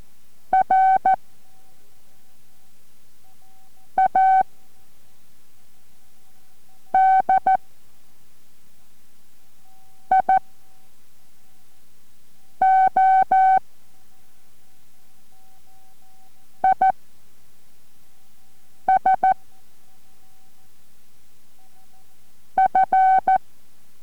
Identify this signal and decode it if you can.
Morse code, heard as a keyed tone, about 14 wpm, 760 Hz. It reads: R A D I O I S F